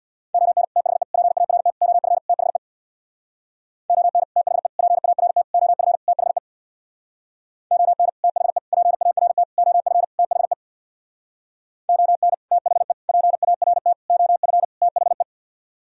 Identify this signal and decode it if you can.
Morse code, heard as a keyed tone, about 64 wpm, 700 Hz. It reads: ON THE PART OF THE ON THE PART OF THE ON THE PART OF THE ON THE PART OF THE